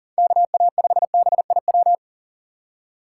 Morse code, heard as a keyed tone, about 40 wpm, 700 Hz. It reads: KA5BIW
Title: USA Call Signs - 6 Characters - 2x Character Spacing - 40wpm